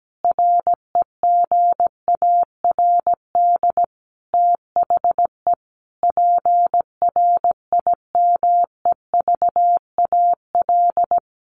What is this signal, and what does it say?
Morse code, heard as a keyed tone, about 17 wpm, 700 Hz. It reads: REGARD THE PRIMEVAL
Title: A Sound Of Thunder 17wpm